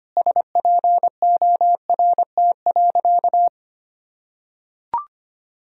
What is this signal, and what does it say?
Morse code, heard as a keyed tone, about 25 wpm, 700 Hz. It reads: SPORT.
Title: Sentences from Top 1000 Words - Review Part 4 - 25wpm